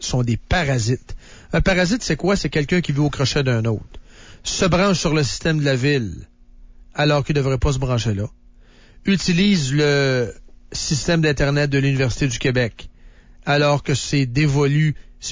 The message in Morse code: none